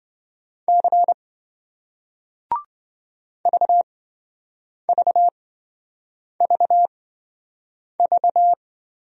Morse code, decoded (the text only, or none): C VVVV